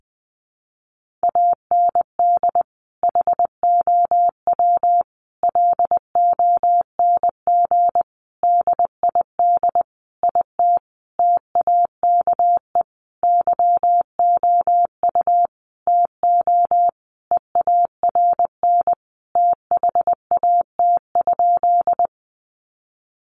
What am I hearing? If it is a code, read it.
Morse code, heard as a keyed tone, about 20 wpm, 700 Hz. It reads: AND HOW LONG DID IT TAKE YOU TO EARN THAT?
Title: Sentences from Top 1500 Words 20wpm